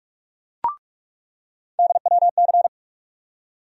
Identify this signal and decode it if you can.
Morse code, heard as a keyed tone, about 42 wpm, 700 Hz. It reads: DWC
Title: USA Call Sign Postfixes - 3 Letters 45wpm